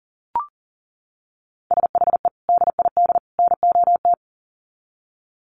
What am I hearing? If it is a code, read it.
Morse code, heard as a keyed tone, about 38 wpm, 700 Hz. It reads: SHE DID NOT
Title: Sets of 3 Words 40wpm